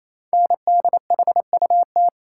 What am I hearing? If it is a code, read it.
Morse code, heard as a keyed tone, about 27 wpm, 700 Hz. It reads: NDHUT
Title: Code Groups - 5 Letters - 28wpm